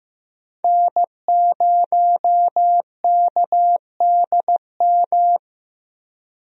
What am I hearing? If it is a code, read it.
Morse code, heard as a keyed tone, about 15 wpm, 700 Hz. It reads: N0KDM
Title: USA Call Signs - 5 Characters - Repeated 2x - 15wpm